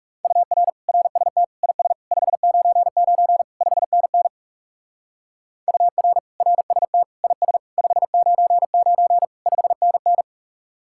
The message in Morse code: UR RST IS 599 5NN UR RST IS 599 5NN

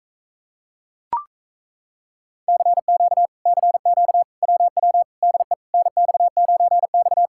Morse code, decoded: CQ CQ WW DE NX9X

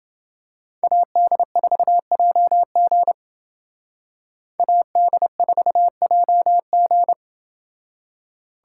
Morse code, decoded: AD4JG AD4JG

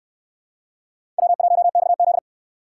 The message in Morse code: W1XZ

